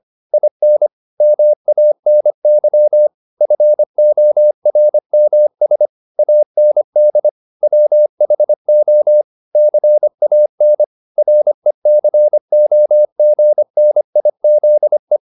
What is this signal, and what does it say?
Morse code, heard as a keyed tone, about 25 wpm, 600 Hz. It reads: IN MANY FORMS AND WHO CAN RECOGNIZE